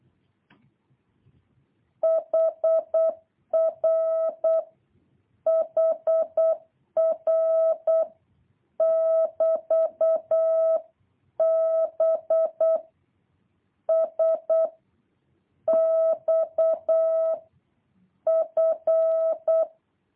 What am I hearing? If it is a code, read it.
Morse code, heard as a keyed tone, about 8 wpm, 650 Hz. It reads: HR HR <BT> B S X F
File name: jh8cbh / 2023cw